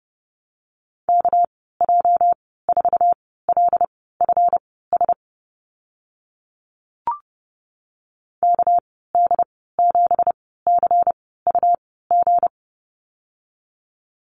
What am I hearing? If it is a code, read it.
Morse code, heard as a keyed tone, about 30 wpm, spaced as for 17 wpm, 700 Hz. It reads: KJ4LFS KD7CUG
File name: Lesson 37g - Letter x - Call Signs 30wpm